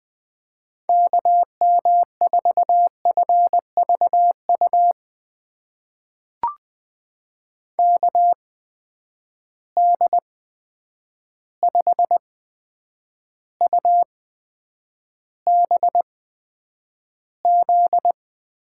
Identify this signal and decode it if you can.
Morse code, heard as a keyed tone, about 20 wpm, spaced as for 9 wpm, 700 Hz. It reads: KM4FVU KD5UBZ
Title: USA Call Signs - 6 Characters - 8x Character Spacing - 20wpm